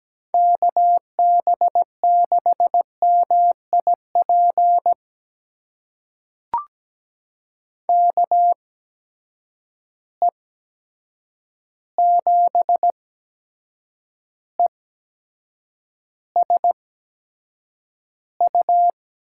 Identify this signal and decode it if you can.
Morse code, heard as a keyed tone, about 17 wpm, spaced as for 7 wpm, 700 Hz. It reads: KB6MIP KE7ESU
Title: USA Call Signs - 6 Characters - 8x Character Spacing - 17wpm